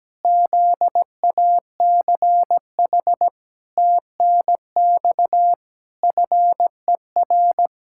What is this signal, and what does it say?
Morse code, heard as a keyed tone, about 17 wpm, 700 Hz. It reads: ZACH TNX FER
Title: K1USN SST QSO Practice 17wpm